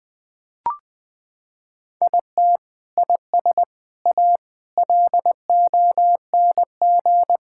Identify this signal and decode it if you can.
Morse code, heard as a keyed tone, about 20 wpm, 700 Hz. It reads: IT IS A LONG
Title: Sentences from Top 1000 Words - Review Part 5 - 20wpm